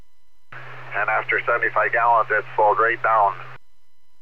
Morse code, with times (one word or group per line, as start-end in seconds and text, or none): none